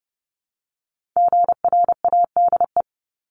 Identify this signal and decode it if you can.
Morse code, heard as a keyed tone, about 30 wpm, 700 Hz. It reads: GRADE
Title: Multi-letter Phonemes 30wpm